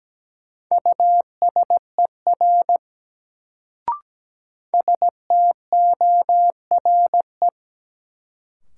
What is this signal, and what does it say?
Morse code, heard as a keyed tone, about 17 wpm, 700 Hz. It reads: USER STORE
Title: Sentences from Top 1000 Words - Review Part 5 - New Word Drill - 17wpm